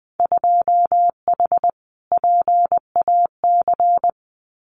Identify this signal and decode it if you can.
Morse code, heard as a keyed tone, about 20 wpm, 700 Hz. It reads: 2H PAC
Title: Winter Field Day Exchanges Encore 20wpm